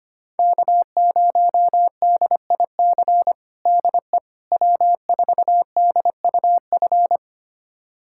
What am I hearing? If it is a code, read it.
Morse code, heard as a keyed tone, about 25 wpm, 700 Hz. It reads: K0DIC DE W4DUF
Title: USA Callsign DE Callsign - 5 Characters - 25wpm